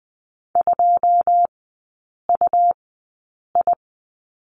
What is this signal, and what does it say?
Morse code, heard as a keyed tone, about 20 wpm, spaced as for 8 wpm, 700 Hz. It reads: 2UI
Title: 100 Lesson 17e - Letter u - Three Characters 20wpm